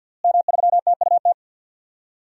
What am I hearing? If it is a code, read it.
Morse code, heard as a keyed tone, about 48 wpm, 700 Hz. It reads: M3TUT
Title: DXCC-100 Call Signs - 5 Characters - Repeated 2x - 50wpm